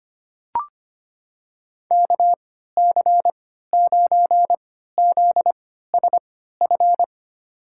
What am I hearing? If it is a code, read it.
Morse code, heard as a keyed tone, about 25 wpm, spaced as for 14 wpm, 700 Hz. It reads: KC9ZSF
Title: USA Call Signs - 3x Character Spacing - 25wpm